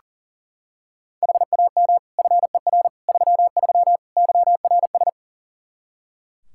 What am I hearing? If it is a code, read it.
Morse code, heard as a keyed tone, about 40 wpm, 700 Hz. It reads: HAM FER 33 YRS
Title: Practice QSO - Exchange 3 - More Info - 40wpm